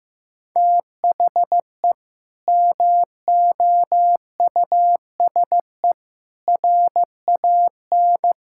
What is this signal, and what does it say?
Morse code, heard as a keyed tone, about 15 wpm, 700 Hz. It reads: THE MOUSE RAN
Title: Sentences from Top 1000 Words - Review Part 5 - 15wpm